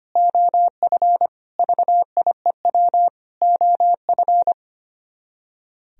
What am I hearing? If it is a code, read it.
Morse code, heard as a keyed tone, about 24 wpm, 700 Hz. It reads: OF VIEW OF
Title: Speed-Racing - Sets of 5 Words 25wpm